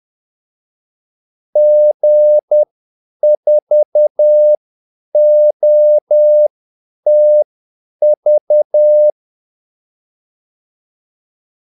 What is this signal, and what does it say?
Morse code, heard as a keyed tone, about 10 wpm, spaced as for 8 wpm, 600 Hz. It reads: G4OTV
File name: Callsigns-10-8wpm